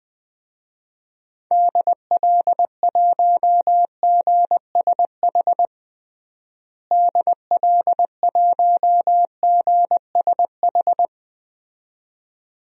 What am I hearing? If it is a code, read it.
Morse code, heard as a keyed tone, about 20 wpm, 700 Hz. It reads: DL1GSH DL1GSH
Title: DXCC-100 Call Signs - 6 Characters - Repeated 2x - 20wpm